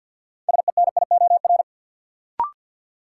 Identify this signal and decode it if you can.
Morse code, heard as a keyed tone, about 52 wpm, 700 Hz. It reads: SENIOR
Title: Sets of 5 Words Encore - New Word Drill - 50wpm